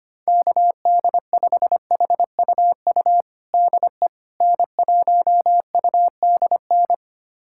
Morse code, heard as a keyed tone, about 25 wpm, 700 Hz. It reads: KD5HUU DE N1UDN
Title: USA Callsign DE Callsign 25wpm